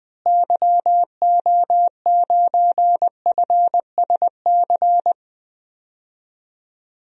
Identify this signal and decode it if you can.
Morse code, heard as a keyed tone, about 20 wpm, 700 Hz. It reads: YO9FSC
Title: DXCC-100 Call Signs - 6 Characters - Repeated 2x - 20wpm